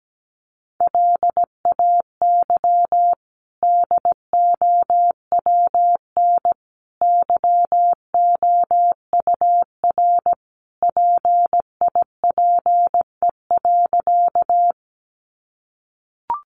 Morse code, heard as a keyed tone, about 17 wpm, 700 Hz. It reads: LAY DOWN YOUR PIPE.
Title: Sentences from Top 1500 Words 17wpm